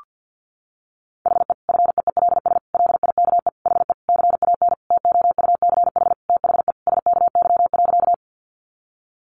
Morse code, heard as a keyed tone, about 50 wpm, 700 Hz. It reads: HE FEELS LIKE HE CAN TOUCH THE SKY.
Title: Additional Easy Phrases 50wpm